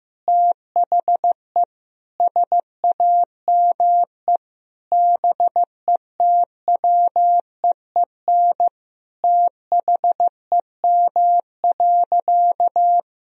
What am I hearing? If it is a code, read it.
Morse code, heard as a keyed tone, about 16 wpm, 700 Hz. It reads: THE SAME BETWEEN THEM.
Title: Sentences from Top 1000 Words - Review Part 1 - 15wpm